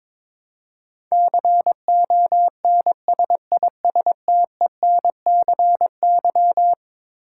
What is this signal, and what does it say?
Morse code, heard as a keyed tone, about 22 wpm, 700 Hz. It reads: CONSISTENCY
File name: practice / 22wpm / Eleven Letter Words - 22wpm